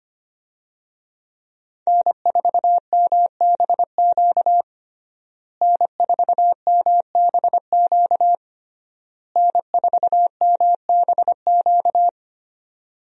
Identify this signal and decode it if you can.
Morse code, heard as a keyed tone, about 25 wpm, 700 Hz. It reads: N4MBQ N4MBQ N4MBQ
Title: USA Call Signs - 5 Characters - Repeated 3x - 25wpm